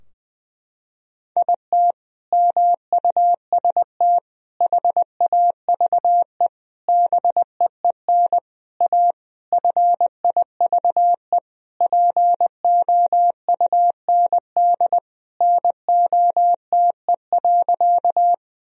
Morse code, heard as a keyed tone, about 21 wpm, 700 Hz. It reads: IT MUST HAVE BEEN A FIVE POUND NOTE.
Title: Sentences from Top 1000 Words - Review Part 2 - 20wpm